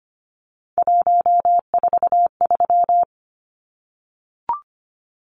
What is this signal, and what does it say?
Morse code, heard as a keyed tone, about 25 wpm, 700 Hz. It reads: 143
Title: Numbers - 3-Digit 25wpm